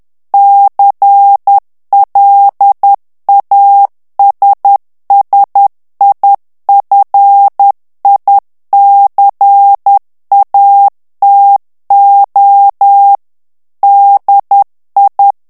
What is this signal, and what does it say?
Morse code, heard as a keyed tone, about 10 wpm, 800 Hz. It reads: CLASSIFICATO DI
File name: ITALIEN texte clair 2 vitesse 10 mots minute